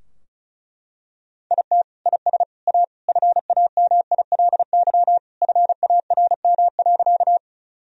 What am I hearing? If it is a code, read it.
Morse code, heard as a keyed tone, about 34 wpm, 700 Hz. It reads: IT IS A FAMILY FARM.
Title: Sentences from Top 1000 Words - Review Part 2 - 35wpm